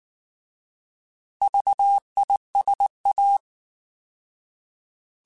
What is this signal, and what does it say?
Morse code, heard as a keyed tone, about 19 wpm, 800 Hz. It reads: VISA